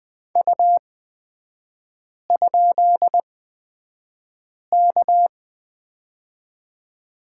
Morse code, heard as a keyed tone, about 20 wpm, spaced as for 5 wpm, 700 Hz. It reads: U?K
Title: 224 Lesson 36e - BT - Three Characters 20wpm